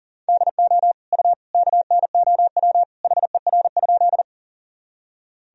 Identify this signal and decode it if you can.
Morse code, heard as a keyed tone, about 37 wpm, 700 Hz. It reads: DO U KNOW HER?
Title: Sentences from Top 100 Words and Abbreviations 40wpm